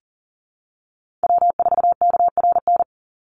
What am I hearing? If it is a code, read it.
Morse code, heard as a keyed tone, about 38 wpm, 700 Hz. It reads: W4KRN
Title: NA SOTA Chaser Callsigns 40wpm